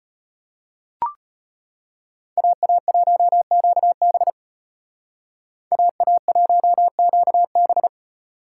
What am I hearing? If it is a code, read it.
Morse code, heard as a keyed tone, about 32 wpm, 700 Hz. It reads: AA1QB AA1QB